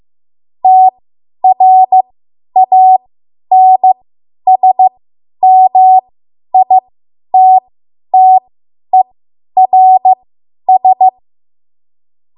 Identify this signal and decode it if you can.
Morse code, heard as a keyed tone, about 15 wpm, spaced as for 10 wpm, 750 Hz. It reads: TRANSMITTERS
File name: Morse / 10WPM